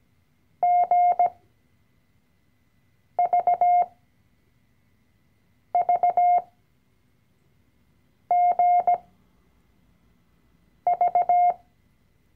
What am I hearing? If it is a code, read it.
Morse code, heard as a keyed tone, about 17 wpm, spaced as for 4 wpm, 710 Hz. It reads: GVVGV
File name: CW-Online-Kurs / Anfänger / Lektion 012